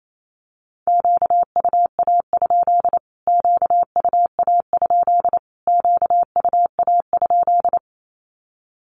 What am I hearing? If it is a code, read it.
Morse code, heard as a keyed tone, about 28 wpm, 700 Hz. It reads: QUA? QUA? QUA?
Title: Q-Codes Repeated 3x 28wpm